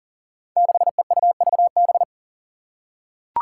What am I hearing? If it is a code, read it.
Morse code, heard as a keyed tone, about 40 wpm, 700 Hz. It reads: 6EUVB
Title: Five Challenging Letter-Number Combinations 40wpm